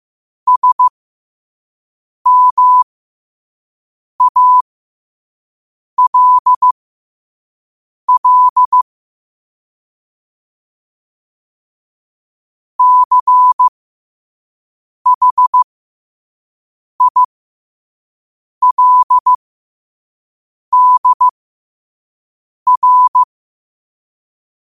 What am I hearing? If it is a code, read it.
Morse code, heard as a keyed tone, about 15 wpm, spaced as for 5 wpm, 1000 Hz. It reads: SMALL CHILDR
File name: qotdfiles / 5wpm